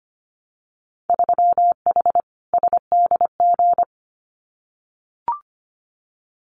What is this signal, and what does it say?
Morse code, heard as a keyed tone, about 25 wpm, 700 Hz. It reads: 3H SDG
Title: Winter Field Day Exchanges Encore 25wpm